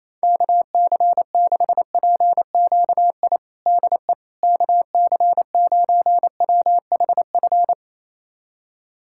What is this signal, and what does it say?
Morse code, heard as a keyed tone, about 27 wpm, 700 Hz. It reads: KC6PQI DE KC9WHF
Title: USA Callsign DE Callsign - 6 Characters - 28wpm